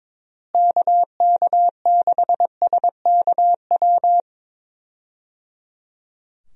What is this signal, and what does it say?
Morse code, heard as a keyed tone, about 22 wpm, 700 Hz. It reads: KK6SKW
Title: USA Call Signs - 6 Characters - Repeated 3x - 22wpm